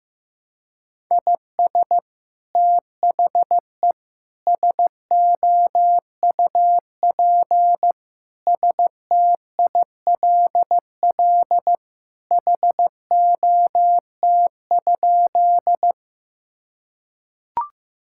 Morse code, heard as a keyed tone, about 15 wpm, 700 Hz. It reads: IS THE SOUP STILL HOT?
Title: Sentences from Top 1000 Words - Review Part 6 - 15wpm